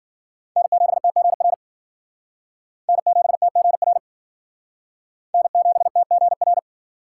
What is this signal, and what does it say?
Morse code, heard as a keyed tone, about 52 wpm, 700 Hz. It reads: N7TGR N7TGR N7TGR